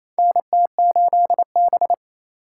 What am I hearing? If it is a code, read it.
Morse code, heard as a keyed tone, about 28 wpm, 700 Hz. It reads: NT8B
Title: USA Call Signs - 2x Character Spacing - 28wpm